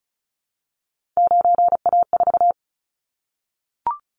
Morse code, heard as a keyed tone, about 36 wpm, 700 Hz. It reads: 9A4
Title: Callsign Prefixes for Top 25 DXCC - 35wpm